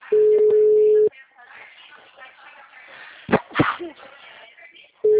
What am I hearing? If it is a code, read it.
Ringback tone (Europe, 425 Hz), with speech.
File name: out- -2017- - - .62118